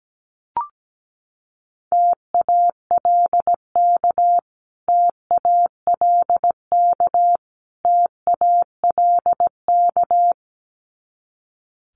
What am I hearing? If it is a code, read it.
Morse code, heard as a keyed tone, about 17 wpm, 700 Hz. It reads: TALK TALK TALK
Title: Top 500 Words Repeated 17wpm